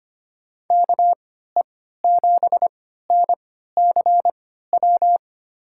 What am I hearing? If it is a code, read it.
Morse code, heard as a keyed tone, about 25 wpm, spaced as for 14 wpm, 700 Hz. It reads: KE7NCW
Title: USA Call Signs - 6 Characters - 3x Character Spacing - 25wpm